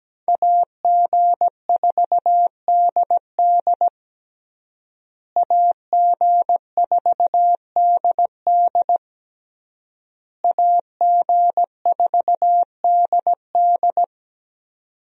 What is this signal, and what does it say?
Morse code, heard as a keyed tone, about 17 wpm, 700 Hz. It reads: AG4DD AG4DD AG4DD
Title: USA Call Signs - 5 Characters - Repeated 3x - 17wpm